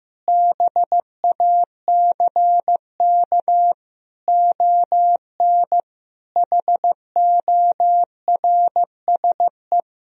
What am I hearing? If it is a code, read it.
Morse code, heard as a keyed tone, about 15 wpm, 700 Hz. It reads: BACK ON HORSE
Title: Sentences from Top 1000 Words - Review Part 3 - 15wpm